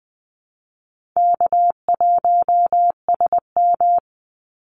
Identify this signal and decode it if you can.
Morse code, heard as a keyed tone, about 20 wpm, 700 Hz. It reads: K1SM
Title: Call Signs Repeated 3x - 20wpm